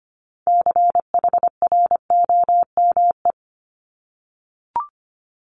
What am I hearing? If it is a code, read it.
Morse code, heard as a keyed tone, about 25 wpm, 700 Hz. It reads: CHROME
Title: Multi-letter Phonemes 25wpm